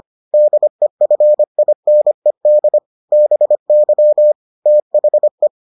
Morse code, heard as a keyed tone, about 25 wpm, 600 Hz. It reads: DEFINED BY THE